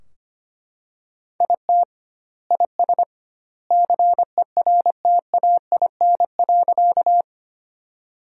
Morse code, heard as a keyed tone, about 25 wpm, 700 Hz. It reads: IT IS CERTAIN.
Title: 2x Word-Spacing - Sentences from Top 500 Words - 25wpm